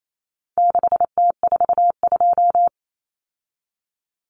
Morse code, heard as a keyed tone, about 28 wpm, 700 Hz. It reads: 6T42
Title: Common Cut Numbers - 4-Digit 28wpm